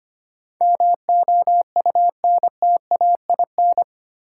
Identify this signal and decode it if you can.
Morse code, heard as a keyed tone, about 25 wpm, 700 Hz. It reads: MOUNTAIN
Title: Sentences from Top 1300 Words - New Word Drill Alt - 25wpm